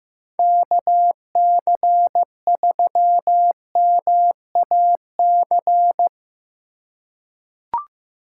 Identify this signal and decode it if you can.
Morse code, heard as a keyed tone, about 15 wpm, 700 Hz. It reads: KC3MAC
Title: USA Call Signs - 6 Characters - 4x Character Spacing - 15wpm